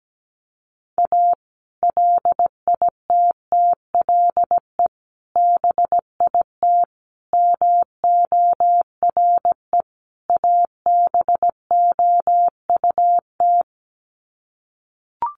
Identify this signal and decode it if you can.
Morse code, heard as a keyed tone, about 17 wpm, 700 Hz. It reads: A LITTLE BIT MORE ABOUT
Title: Sets of 5 Words Encore2 17wpm